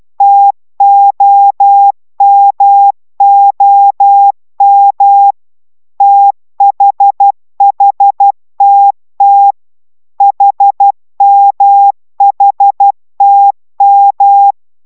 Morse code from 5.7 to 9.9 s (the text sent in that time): THHTT